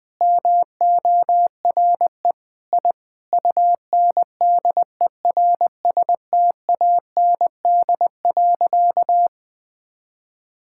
Morse code, heard as a keyed tone, about 20 wpm, 700 Hz. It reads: MORE I UNDERSTAND.
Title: Sentences with 9-Word Phrases - 20wpm